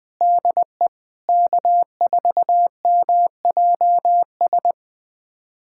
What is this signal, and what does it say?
Morse code, heard as a keyed tone, about 20 wpm, 700 Hz. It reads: DE K4MJS
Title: USA Callsign DE Callsign - 5 Characters - 20wpm